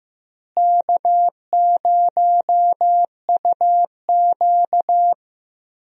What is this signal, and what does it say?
Morse code, heard as a keyed tone, about 15 wpm, 700 Hz. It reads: K0UQ
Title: USA Call Signs - 4 Characters - Repeated 3x - 15wpm